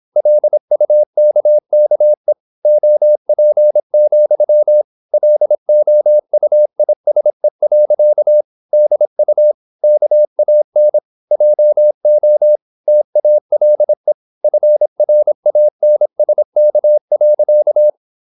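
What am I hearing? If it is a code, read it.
Morse code, heard as a keyed tone, about 26 wpm, 600 Hz. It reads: LUKKE OP, LOUISE. DU KAN JO TALE FRANSK.
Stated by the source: High-speed (QRQ) Morse at 26 wpm.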